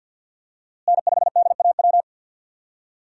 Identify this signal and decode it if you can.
Morse code, heard as a keyed tone, about 48 wpm, 700 Hz. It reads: N5DAW
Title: USA Call Signs - 5 Characters - Repeated 3x - 50wpm